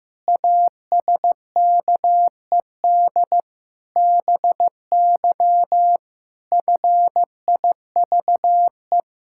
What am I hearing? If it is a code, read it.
Morse code, heard as a keyed tone, about 16 wpm, 700 Hz. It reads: ASKED BY FIVE
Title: Pangram Challenge v2 - 15wpm